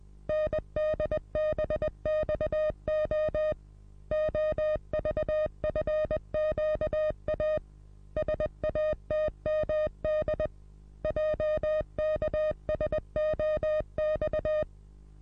Morse code, heard as a keyed tone, about 20 wpm, 620 Hz. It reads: NDBXO OVFQA SATMD JKSOX
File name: repos / Morsekurs / DJ4UF Morsen27